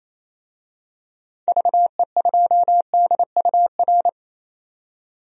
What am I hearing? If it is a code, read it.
Morse code, heard as a keyed tone, about 28 wpm, 700 Hz. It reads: VE2DUR
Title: DXCC-100 Call Signs - 6 Characters - 28wpm